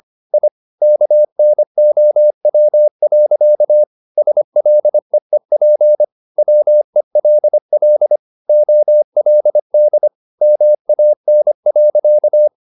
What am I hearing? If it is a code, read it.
Morse code, heard as a keyed tone, about 25 wpm, 600 Hz. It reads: I KNOW. SLEEP WELL OLD MAN.
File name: the old man and the sea part 1 25wpm